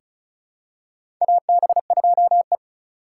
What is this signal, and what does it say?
Morse code, heard as a keyed tone, about 36 wpm, 700 Hz. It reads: AB2E
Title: USA Call Signs - 4 Characters - 4x Character Spacing - 35wpm